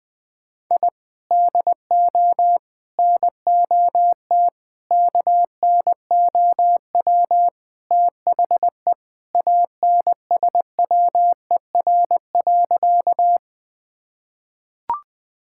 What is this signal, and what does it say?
Morse code, heard as a keyed tone, about 20 wpm, 700 Hz. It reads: I DO NOT KNOW THE ANSWER.
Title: Sentences with 5-Word Phrases - Spoken Beginning - 20wpm